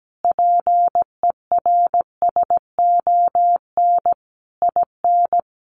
Morse code, heard as a keyed tone, about 17 wpm, 700 Hz. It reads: PERSON IN
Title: Sentences from Top 1200 Words 17wpm